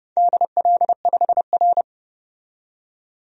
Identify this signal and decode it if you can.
Morse code, heard as a keyed tone, about 29 wpm, 700 Hz. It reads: DL5R
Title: DXCC-100 Call Signs - 4 Characters - 30wpm